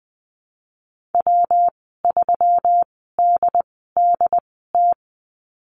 Morse code, heard as a keyed tone, about 20 wpm, spaced as for 15 wpm, 700 Hz. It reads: W3DDT
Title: 154 Lesson 25g - Letter p - Call Signs 20wpm